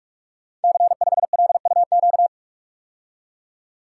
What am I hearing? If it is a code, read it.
Morse code, heard as a keyed tone, about 46 wpm, 700 Hz. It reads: C5LUQ
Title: Code Groups - 5 Letter:Numbers - 45wpm